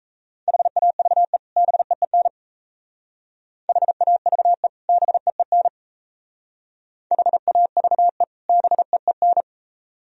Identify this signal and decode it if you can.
Morse code, heard as a keyed tone, about 34 wpm, 700 Hz. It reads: HAVE BEEN HAVE BEEN HAVE BEEN
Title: Speed-Racing - Sets of 2 Words 28wpm